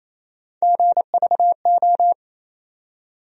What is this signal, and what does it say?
Morse code, heard as a keyed tone, about 27 wpm, 700 Hz. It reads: GVO
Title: USA Call Sign Postfixes - 3 Letters 28wpm